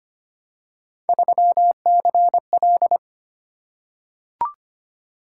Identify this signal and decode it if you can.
Morse code, heard as a keyed tone, about 25 wpm, 700 Hz. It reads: 3CL
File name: Three Letter-Number Combinations - 25wpm